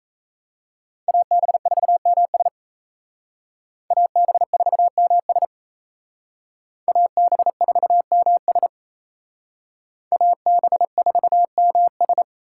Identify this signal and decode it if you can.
Morse code, heard as a keyed tone, about 35 wpm, 700 Hz. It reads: AB4MS AB4MS AB4MS AB4MS
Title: Speed Racing - USA 2x2 Call Signs 28wpm